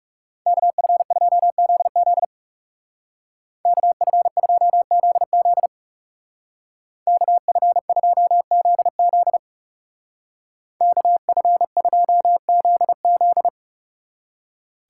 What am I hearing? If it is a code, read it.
Morse code, heard as a keyed tone, about 38 wpm, 700 Hz. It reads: KF2ZZ KF2ZZ KF2ZZ KF2ZZ